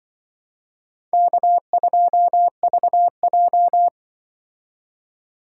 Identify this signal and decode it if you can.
Morse code, heard as a keyed tone, about 24 wpm, 700 Hz. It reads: K2VJ
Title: Speed Racing - USA 1x2 Call Signs 20wpm